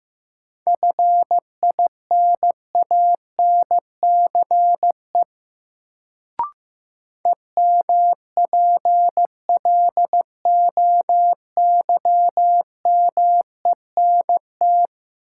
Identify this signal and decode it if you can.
Morse code, heard as a keyed tone, about 15 wpm, 700 Hz. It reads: FINANCE EMPLOYMENT
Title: Sentences from Top 1500 Words - New Word Drill - 15wpm